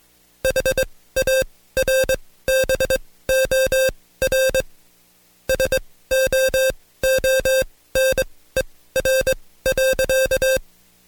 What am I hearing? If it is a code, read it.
Morse code, heard as a keyed tone, about 21 wpm, spaced as for 15 wpm, 530 Hz. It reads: HARBOR SOONER.
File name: Code Quick / AudioDownld2 / Seg134-Stepper7 15WPM